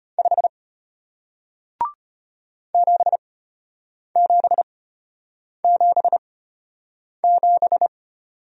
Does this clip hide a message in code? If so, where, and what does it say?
Yes: Morse code from 0.2 to 7.9 s, heard as a keyed tone, about 34 wpm, spaced as for 8 wpm, 700 Hz. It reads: 5 7777